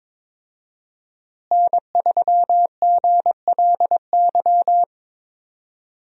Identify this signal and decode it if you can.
Morse code, heard as a keyed tone, about 22 wpm, 700 Hz. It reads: N3GLY
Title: USA Call Signs - 5 Characters - Repeated 3x - 22wpm